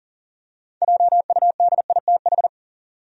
Morse code, heard as a keyed tone, about 40 wpm, 700 Hz. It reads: JUDITH
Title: USA Common Names - 40wpm